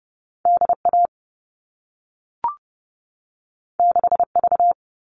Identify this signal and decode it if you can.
Morse code, heard as a keyed tone, about 30 wpm, 700 Hz. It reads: DA 6V
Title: All Cut Numbers - 2-Digit 30wpm